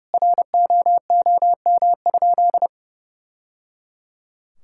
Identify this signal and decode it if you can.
Morse code, heard as a keyed tone, about 28 wpm, 700 Hz. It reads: ROOM?
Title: Sentences from Top 1000 Words - Review Part 6 - 30wpm